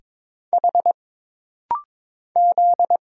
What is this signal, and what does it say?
Morse code, heard as a keyed tone, about 22 wpm, 700 Hz. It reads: H Z
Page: This rapid-fire timing is geared to push your ICR (Instant Character Recognition) skills.